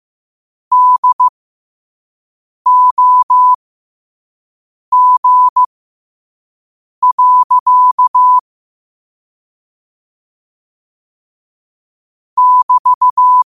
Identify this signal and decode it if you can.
Morse code, heard as a keyed tone, about 15 wpm, spaced as for 5 wpm, 1000 Hz. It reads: DOG. <BT>
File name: qotdfiles / 5wpm